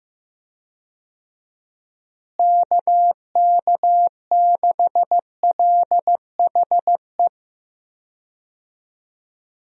Morse code, heard as a keyed tone, about 15 wpm, 700 Hz. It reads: KK6LHE
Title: USA Call Signs - 6 Characters - Repeated 2x - 15wpm